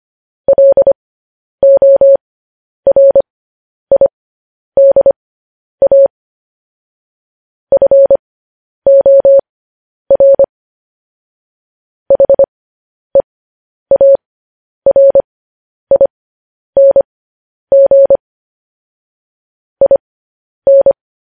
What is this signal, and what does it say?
Morse code, heard as a keyed tone, about 25 wpm, spaced as for 10 wpm, 550 Hz. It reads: LORIDA FOR HEARING IN